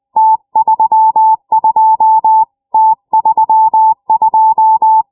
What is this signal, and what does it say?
Morse code, heard as a keyed tone, about 20 wpm, 880 Hz. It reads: T32 T32
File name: iw0ffk / 70cm / 2006 06 04 ok1teh iq0os